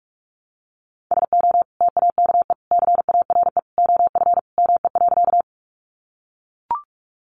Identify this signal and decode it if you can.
Morse code, heard as a keyed tone, about 44 wpm, 700 Hz. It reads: SO TAKE CARE OF ME.
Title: Sentences from Top 300 Words 45wpm